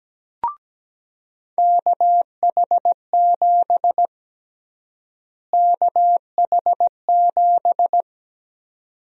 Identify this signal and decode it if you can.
Morse code, heard as a keyed tone, about 17 wpm, 700 Hz. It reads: KH7 KH7
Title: USA Call Sign Prefixes - Repeated 2x - 17wpm